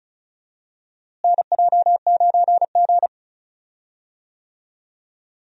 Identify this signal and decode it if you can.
Morse code, heard as a keyed tone, about 36 wpm, 700 Hz. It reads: NJ9G
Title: USA Call Signs - 4 Characters - Repeated 2x - 35wpm